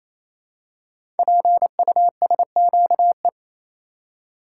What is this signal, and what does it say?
Morse code, heard as a keyed tone, about 27 wpm, 700 Hz. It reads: PUSQE
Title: Code Groups - 5 Letters - 28wpm